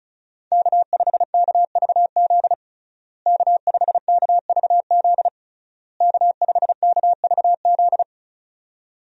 Morse code, heard as a keyed tone, about 36 wpm, 700 Hz. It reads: K5KVZ K5KVZ K5KVZ
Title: USA Call Signs - 5 Characters - Repeated 3x - 35wpm